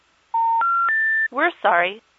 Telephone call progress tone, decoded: special information tone